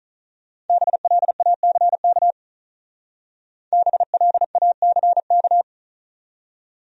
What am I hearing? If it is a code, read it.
Morse code, heard as a keyed tone, about 45 wpm, 700 Hz. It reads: BLACK BLACK